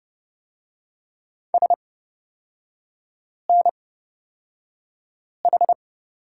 Morse code, heard as a keyed tone, about 31 wpm, spaced as for 5 wpm, 700 Hz. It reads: SNH
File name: Lesson 36e - BT - Three Characters 30wpm